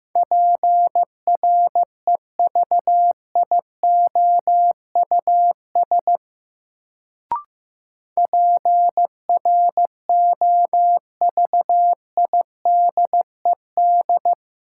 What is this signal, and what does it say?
Morse code, heard as a keyed tone, about 15 wpm, 700 Hz. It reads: PREVIOUS PROVIDED
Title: Sentences from Top 1500 Words - New Word Drill - 15wpm